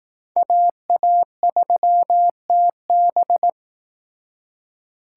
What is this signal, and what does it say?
Morse code, heard as a keyed tone, about 19 wpm, 700 Hz. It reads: AA3TB